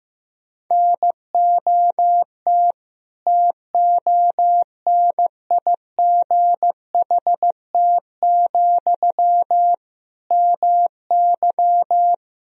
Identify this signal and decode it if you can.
Morse code, heard as a keyed tone, about 15 wpm, 700 Hz. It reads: NOT TONIGHT, MY
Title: Sentences from Top 1000 Words - Review Part 3 - 15wpm